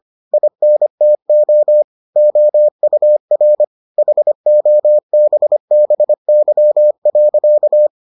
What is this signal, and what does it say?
Morse code, heard as a keyed tone, about 25 wpm, 600 Hz. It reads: INTO OUR HOBBY.